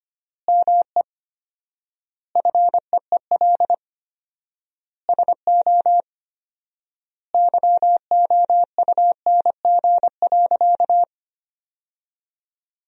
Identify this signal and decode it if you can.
Morse code, heard as a keyed tone, about 25 wpm, 700 Hz. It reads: ME FEEL SO YOUNG.
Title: 4x Word-Spacing - Sentences from Top 400 Words - 25wpm